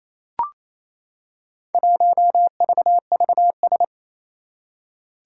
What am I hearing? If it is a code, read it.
Morse code, heard as a keyed tone, about 28 wpm, 700 Hz. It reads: 1VVS
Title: Four Challenging Letter-Number Combinations 28wpm